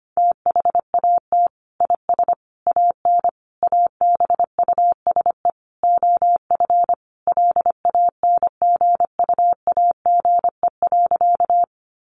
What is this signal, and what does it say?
Morse code, heard as a keyed tone, about 25 wpm, 700 Hz. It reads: THAT IS AN ABUSE OF LANGUAGE.
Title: Sentences from Top 1300 Words 25wpm